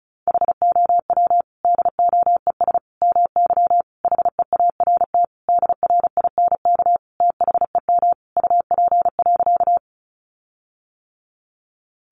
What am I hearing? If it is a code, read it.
Morse code, heard as a keyed tone, about 34 wpm, 700 Hz. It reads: HOW DOES MY HEART DRINK THEM UP.
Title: Sentences from Top 1500 Words 35wpm